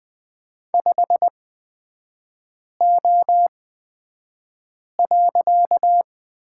Morse code, heard as a keyed tone, about 20 wpm, spaced as for 5 wpm, 700 Hz. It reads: 5O.
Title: 255 Lesson 41e - Period - Three Characters 20wpm